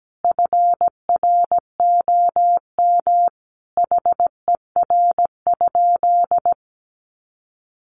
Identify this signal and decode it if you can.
Morse code, heard as a keyed tone, about 17 wpm, 700 Hz. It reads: FROM HER?
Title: Sentences from Top 900 Words 17wpm